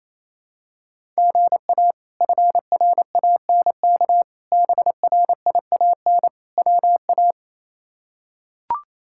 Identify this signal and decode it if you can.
Morse code, heard as a keyed tone, about 28 wpm, 700 Hz. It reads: GA FRANK BRIAN WA
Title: K1USN SST QSO Practice - 28wpm